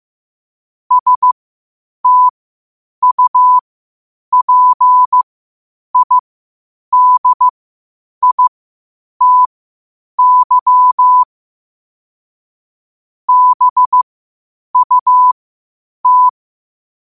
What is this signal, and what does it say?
Morse code, heard as a keyed tone, about 15 wpm, spaced as for 8 wpm, 1000 Hz. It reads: STUPIDITY BUT
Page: 8 wpm morse code quotes